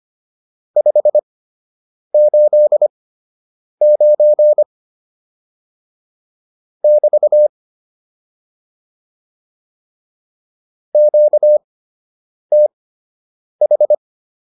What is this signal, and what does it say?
Morse code, heard as a keyed tone, about 25 wpm, spaced as for 8 wpm, 600 Hz. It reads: 589 <BT> QTH